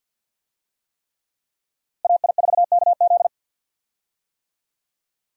Morse code, heard as a keyed tone, about 45 wpm, 700 Hz. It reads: AI4KZ
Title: USA Call Signs - 5 Characters - Repeated 3x - 50wpm